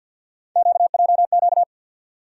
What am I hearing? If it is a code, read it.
Morse code, heard as a keyed tone, about 52 wpm, 700 Hz. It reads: QJQ
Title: USA Call Sign Postfixes - 3 Letters 50wpm